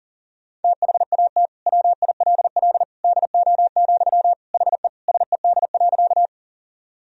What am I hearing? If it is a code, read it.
Morse code, heard as a keyed tone, about 40 wpm, 700 Hz. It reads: THAT WILL DO, HE SED.
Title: Sentences from Top 100 Words and Abbreviations 40wpm